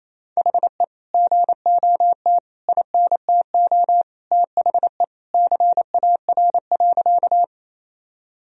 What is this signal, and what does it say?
Morse code, heard as a keyed tone, about 28 wpm, 700 Hz. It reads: HE GOT INTO THE CAR.
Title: Sentences from Top 1000 Words - Review Part 1 - 28wpm